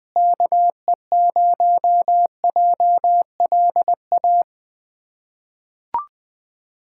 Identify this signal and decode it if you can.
Morse code, heard as a keyed tone, about 20 wpm, 700 Hz. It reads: KE0JLA
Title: USA Callsign DE Callsign - 6 Characters - 20wpm